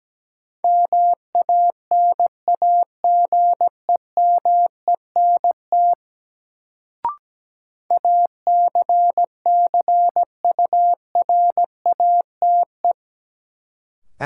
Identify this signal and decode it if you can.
Morse code, heard as a keyed tone, about 17 wpm, 700 Hz. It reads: MANAGEMENT ACCURATE
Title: Sentences from Top 1200 Words - New Word Drill Alt - 17wpm